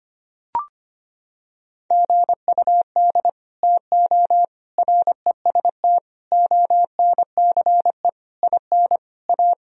Morse code, heard as a keyed tone, about 25 wpm, 700 Hz. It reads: GUD TO REST ONCE IN A
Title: Sentences from Top 500 Words and Abbreviations 25wpm